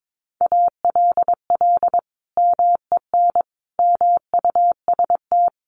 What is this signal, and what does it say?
Morse code, heard as a keyed tone, about 22 wpm, 700 Hz. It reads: ALL MEN MUST
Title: Sentences from Top 1300 Words 22wpm